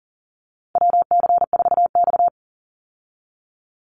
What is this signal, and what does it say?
Morse code, heard as a keyed tone, about 40 wpm, 700 Hz. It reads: WC4X
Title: Call Signs Repeated 3x - 40wpm